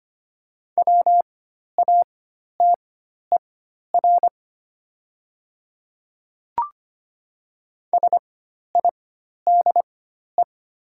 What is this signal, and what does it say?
Morse code, heard as a keyed tone, about 25 wpm, spaced as for 12 wpm, 700 Hz. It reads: WATER SIDE
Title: 4x Character Spacing - Top 100 Words - 25wpm